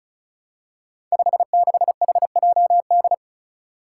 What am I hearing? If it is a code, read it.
Morse code, heard as a keyed tone, about 34 wpm, 700 Hz. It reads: 56HJD
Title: Five Challenging Letter-Number Combinations 35wpm